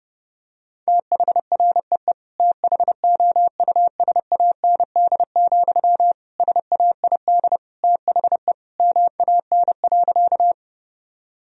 Morse code, heard as a keyed tone, about 30 wpm, 700 Hz. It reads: THREE THOUSAND, SAID THE MAN.
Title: Sentences from Top 1000 Words - Review Part 3 - 30wpm